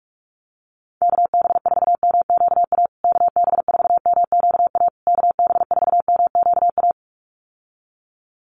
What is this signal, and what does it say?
Morse code, heard as a keyed tone, about 44 wpm, 700 Hz. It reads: KB4MQA KB4MQA KB4MQA
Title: USA Call Signs Repeated 3x 45wpm